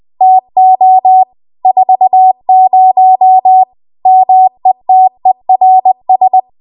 Morse code, heard as a keyed tone, about 20 wpm, 750 Hz. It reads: TO 40 METERS
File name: Archive / 20 WPM / 20WPM